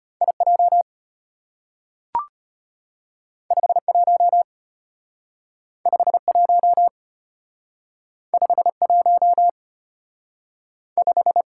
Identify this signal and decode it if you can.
Morse code, heard as a keyed tone, about 34 wpm, 700 Hz. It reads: IJ 51 51 51 5